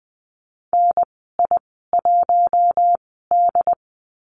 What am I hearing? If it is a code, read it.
Morse code, heard as a keyed tone, about 20 wpm, spaced as for 14 wpm, 700 Hz. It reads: NI1D
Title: 076 Lesson 13g - Letter L - Call Signs 20wpm